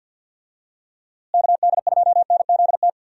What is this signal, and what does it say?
Morse code, heard as a keyed tone, about 46 wpm, 700 Hz. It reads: KD2NZT
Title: USA Call Signs - 6 Characters - Repeated 2x - 50wpm